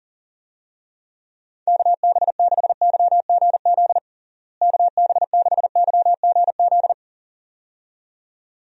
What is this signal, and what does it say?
Morse code, heard as a keyed tone, about 40 wpm, 700 Hz. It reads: KB6YGZ KB6YGZ
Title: USA Call Signs - 6 Characters - Repeated 2x - 40wpm